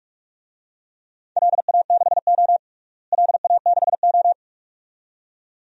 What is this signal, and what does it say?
Morse code, heard as a keyed tone, about 46 wpm, 700 Hz. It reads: LA6O LA6O
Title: DXCC-100 Call Signs - 4 Characters - Repeated 2x - 45wpm